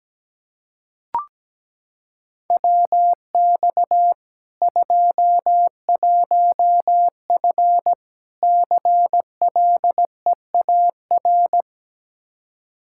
Morse code, heard as a keyed tone, about 17 wpm, 700 Hz. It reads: WX 21F CLEAR
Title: Practice QSO - Exchange 2 - Info - 17wpm